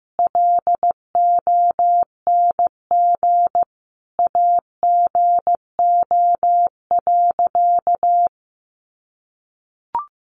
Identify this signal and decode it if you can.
Morse code, heard as a keyed tone, about 14 wpm, 700 Hz. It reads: LONG AGO.
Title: Sentences from Top 500 Words 15wpm